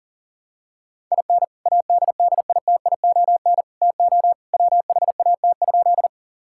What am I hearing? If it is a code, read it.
Morse code, heard as a keyed tone, about 40 wpm, 700 Hz. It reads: IN ADDITION TO WHAT?
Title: Sentences from Top 1000 Words - Review Part 6 - 40wpm